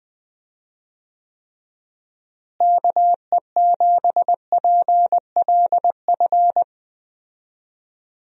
Morse code, heard as a keyed tone, about 20 wpm, 700 Hz. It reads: KE7PLF
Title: USA Call Signs - 6 Characters - Repeated 3x - 20wpm